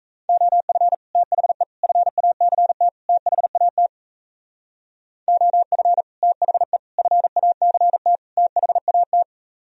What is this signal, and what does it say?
Morse code, heard as a keyed tone, about 35 wpm, 700 Hz. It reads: OF THE FACT THAT OF THE FACT THAT